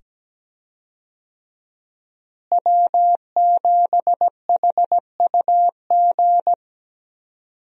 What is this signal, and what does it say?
Morse code, heard as a keyed tone, about 17 wpm, 700 Hz. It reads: W7HUG
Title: USA Call Signs - 5 Characters - Repeated 2x - 17wpm